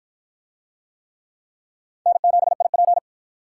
Morse code, heard as a keyed tone, about 53 wpm, 700 Hz. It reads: N7IP